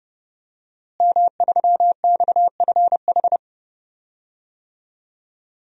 Morse code, heard as a keyed tone, about 30 wpm, 700 Hz. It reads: M3XFH
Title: DXCC-100 Call Signs - 5 Characters - Repeated 2x - 30wpm